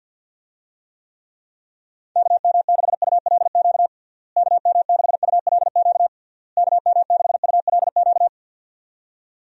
Morse code, heard as a keyed tone, about 52 wpm, 700 Hz. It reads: KM6ULQ KM6ULQ KM6ULQ
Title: USA Call Signs - 6 Characters - Repeated 3x - 50wpm